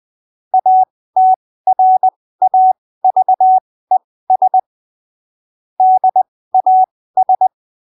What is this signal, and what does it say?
Morse code, heard as a keyed tone, about 20 wpm, spaced as for 15 wpm, 750 Hz. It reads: ATRAVES DAS
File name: arquivos / texto / texto 23 20ppm